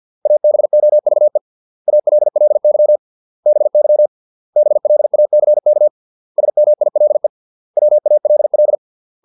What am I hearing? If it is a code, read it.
Morse code, heard as a keyed tone, about 50 wpm, 600 Hz. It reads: ABOVE ALLY BY BLACK SMILE WALL